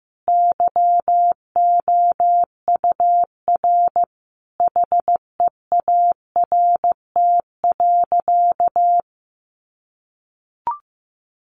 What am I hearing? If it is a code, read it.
Morse code, heard as a keyed tone, about 15 wpm, 700 Hz. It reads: YOUR HEART.
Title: Sentences from Top 1200 Words 15wpm